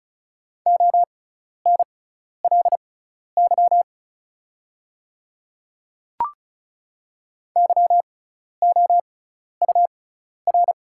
Morse code, heard as a keyed tone, about 36 wpm, spaced as for 12 wpm, 700 Hz. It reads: ONLY YOUR
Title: 6x Character Spacing - Top 300 Words - 35wpm